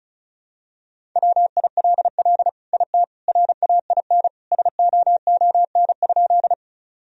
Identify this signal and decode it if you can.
Morse code, heard as a keyed tone, about 36 wpm, 700 Hz. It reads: WILL IT RAIN SOON?
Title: Sentences from Top 1000 Words - Review Part 1 - 35wpm